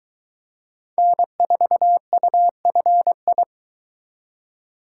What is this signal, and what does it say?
Morse code, heard as a keyed tone, about 22 wpm, 700 Hz. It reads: N4UFI